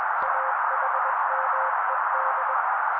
Morse code, heard as a keyed tone, about 21 wpm, 530 Hz. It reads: AHMED
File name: daily / 20wpm noise